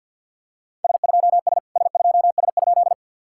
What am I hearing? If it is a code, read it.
Morse code, heard as a keyed tone, about 46 wpm, 690 Hz. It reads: S2S S2S?
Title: SOTA QSO Practice 50wpm